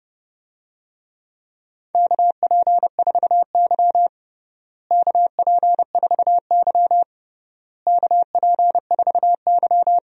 Morse code, heard as a keyed tone, about 30 wpm, 700 Hz. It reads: KP4Y KP4Y KP4Y
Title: DXCC-100 Call Signs - 4 Characters - Repeated 3x - 30wpm